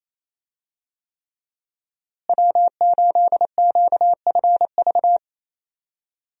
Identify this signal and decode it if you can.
Morse code, heard as a keyed tone, about 28 wpm, 700 Hz. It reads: W8QFV
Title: USA Call Signs - 5 Characters - Repeated 3x - 28wpm